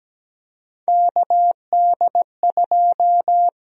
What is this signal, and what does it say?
Morse code, heard as a keyed tone, about 17 wpm, 700 Hz. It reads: KD2
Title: USA Call Sign Prefixes - Repeated 2x - 17wpm